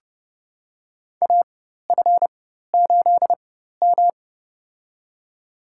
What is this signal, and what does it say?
Morse code, heard as a keyed tone, about 30 wpm, spaced as for 14 wpm, 700 Hz. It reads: AF8M
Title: USA Call Signs - 4 Characters - 4x Character Spacing - 30wpm